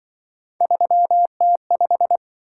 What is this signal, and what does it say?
Morse code, heard as a keyed tone, about 24 wpm, 700 Hz. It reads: 3T5
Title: Speed-Racing - Three Letter-Number Combinations - 20wpm